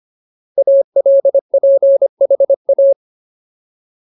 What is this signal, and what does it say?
Morse code, heard as a keyed tone, about 25 wpm, 550 Hz. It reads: ALPHA
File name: Ultimate Word List / 25WPM Ultimate